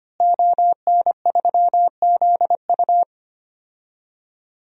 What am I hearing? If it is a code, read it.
Morse code, heard as a keyed tone, about 25 wpm, 700 Hz. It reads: ON3ZU
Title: DXCC-100 Call Signs - 5 Characters - Repeated 3x - 25wpm